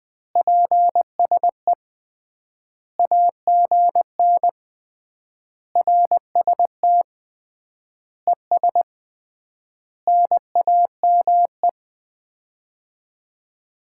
Morse code, heard as a keyed tone, about 21 wpm, 700 Hz. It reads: PSE AGN RST ES NAME
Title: Common QSO Phrases - 3x Word-Spacing - 20wpm